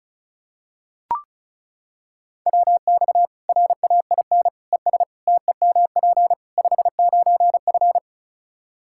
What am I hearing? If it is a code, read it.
Morse code, heard as a keyed tone, about 36 wpm, 700 Hz. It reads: WX RAIN ES TEMP 59F
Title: Practice QSO - Exchange 2 - Info - 35wpm